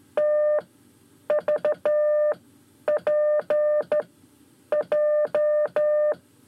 Morse code, heard as a keyed tone, about 12 wpm, 600 Hz. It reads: T V P J
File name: 5 Lezione CW 1